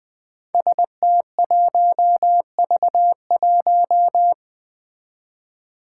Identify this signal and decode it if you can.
Morse code, heard as a keyed tone, about 20 wpm, 700 Hz. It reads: ST1V1
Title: Five Challenging Letter-Number Combinations 20wpm